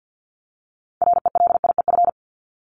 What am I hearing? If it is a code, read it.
Morse code, heard as a keyed tone, about 50 wpm, 700 Hz. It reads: RELIEF
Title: Top 1501-1750 Words 50wpm